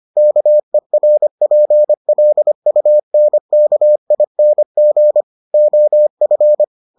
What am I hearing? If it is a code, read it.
Morse code, heard as a keyed tone, about 25 wpm, 600 Hz. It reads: KERPLUNKING OF